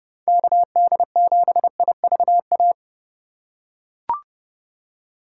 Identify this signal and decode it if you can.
Morse code, heard as a keyed tone, about 31 wpm, 700 Hz. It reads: KD7IVA
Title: USA Call Signs - 2x Character Spacing - 30wpm